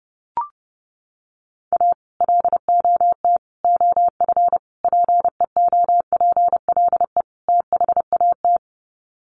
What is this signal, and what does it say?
Morse code, heard as a keyed tone, about 30 wpm, 700 Hz. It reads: A LOT OF PEOPLE THAT
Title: Sets of 5 Words Encore2 30wpm